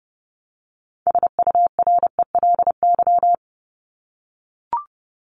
Morse code, heard as a keyed tone, about 29 wpm, 700 Hz. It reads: SURELY
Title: LY Suffix - Top 100 Words with Base Words 30wpm